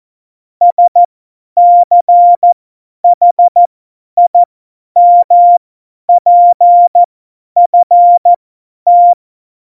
Morse code, heard as a keyed tone, about 14 wpm, spaced as for 10 wpm, 700 Hz. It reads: SCHIMPFT